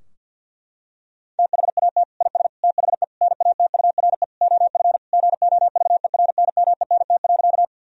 Morse code, heard as a keyed tone, about 51 wpm, 700 Hz. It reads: THAT IS THE NATURE OF GOVERNMENT.
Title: Sentences from Top 1000 Words - Review Part 3 - 50wpm